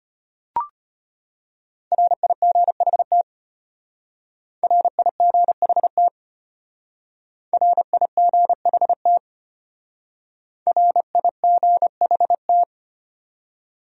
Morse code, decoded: RIGHT RIGHT RIGHT RIGHT

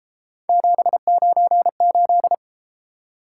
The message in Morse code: 798

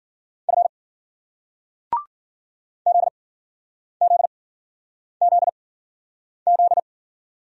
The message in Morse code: F ZZZZ

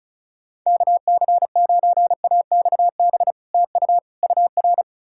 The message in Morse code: KC9AXB TU UR

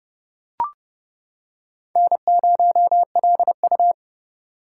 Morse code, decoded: N0LU

